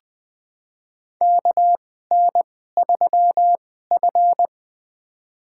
Morse code, heard as a keyed tone, about 20 wpm, spaced as for 14 wpm, 700 Hz. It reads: KN3F
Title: USA Call Signs - 2x Character Spacing - 20wpm